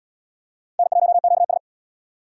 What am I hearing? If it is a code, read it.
Morse code, heard as a keyed tone, about 72 wpm, 700 Hz. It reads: N1QS